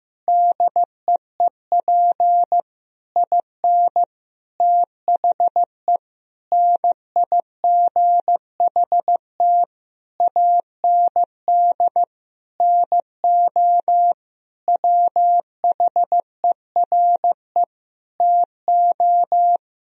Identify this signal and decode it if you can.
Morse code, heard as a keyed tone, about 15 wpm, 700 Hz. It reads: DEEP IN THE NIGHT AND NO WHERE TO
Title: Sentences from Top 1000 Words - Review Part 3 - 15wpm